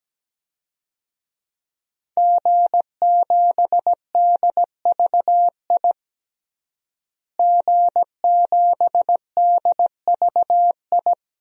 Morse code, heard as a keyed tone, about 17 wpm, 700 Hz. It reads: G7DVI G7DVI
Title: DXCC-100 Call Signs - 5 Characters - Repeated 3x - 17wpm